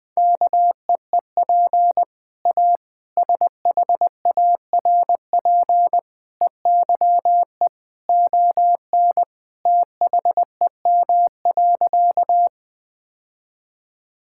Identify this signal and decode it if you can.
Morse code, heard as a keyed tone, about 20 wpm, 700 Hz. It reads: KEEP A SHARP EYE ON THEM.
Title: Sentences from Top 1600 Words 20wpm